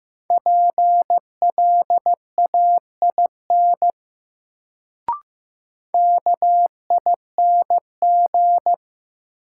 Morse code, heard as a keyed tone, about 16 wpm, 700 Hz. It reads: PLAIN KING
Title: Sentences from Top 400 Words - New Word Drill - 15wpm